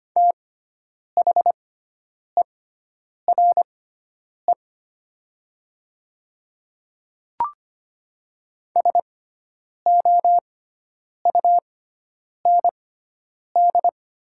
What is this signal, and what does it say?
Morse code, heard as a keyed tone, about 25 wpm, spaced as for 9 wpm, 700 Hz. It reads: THERE SOUND
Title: 6x Character Spacing - Top 100 Words - 25wpm